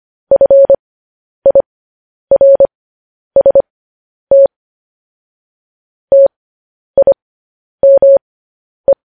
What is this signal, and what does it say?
Morse code, heard as a keyed tone, about 25 wpm, spaced as for 10 wpm, 550 Hz. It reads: FIRST TIME
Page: News Headlines in Morse Code at 10 WPM